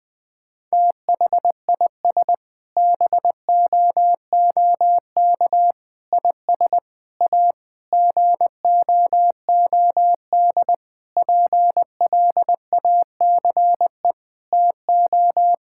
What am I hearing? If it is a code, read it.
Morse code, heard as a keyed tone, about 20 wpm, 700 Hz. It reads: THIS BOOK IS A GOOD PLACE TO
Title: Sentences with 6-Word Phrases - 20wpm